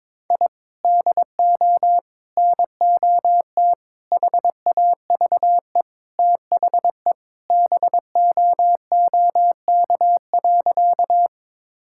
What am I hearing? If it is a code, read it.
Morse code, heard as a keyed tone, about 22 wpm, 700 Hz. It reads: I DO NOT HAVE THE BOOK.
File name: practice / 22wpm / Sentences with 4-Word Phrases - 22wpm